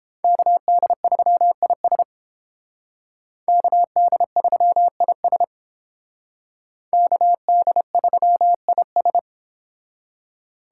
Morse code, heard as a keyed tone, about 28 wpm, 700 Hz. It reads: KD3IS KD3IS KD3IS